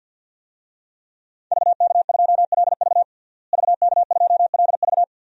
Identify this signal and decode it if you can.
Morse code, heard as a keyed tone, about 50 wpm, 700 Hz. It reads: VK2LV VK2LV
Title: DXCC-100 Call Signs - 5 Characters - Repeated 2x - 50wpm